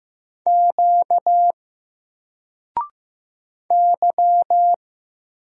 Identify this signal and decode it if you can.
Morse code, heard as a keyed tone, about 15 wpm, 700 Hz. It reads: Q Y
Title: Single Letter-Number 15wpm